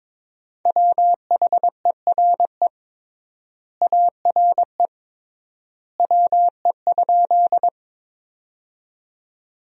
Morse code, heard as a keyed tone, about 22 wpm, 700 Hz. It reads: WHERE ARE WE?
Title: 3x Word-Spacing - Sentences from Top 200 Words - 22wpm